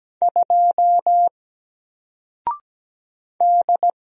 Morse code, heard as a keyed tone, about 17 wpm, 700 Hz. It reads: 2 D
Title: Single Character (Letter:Number:Punctuation:Prosign) Alt 17wpm